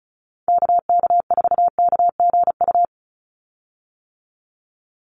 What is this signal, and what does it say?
Morse code, heard as a keyed tone, about 34 wpm, 700 Hz. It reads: KK4KGU
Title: USA Call Signs 35wpm